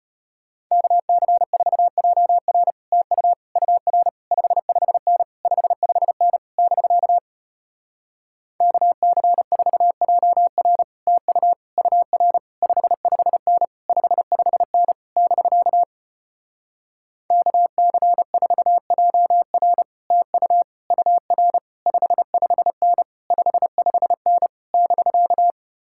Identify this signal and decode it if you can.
Morse code, heard as a keyed tone, about 33 wpm, 700 Hz. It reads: KC4JR TU UR 55N 55N <BK> KC4JR TU UR 55N 55N <BK> KC4JR TU UR 55N 55N <BK>